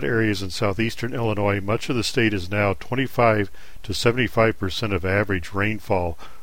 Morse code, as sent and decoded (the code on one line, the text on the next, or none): none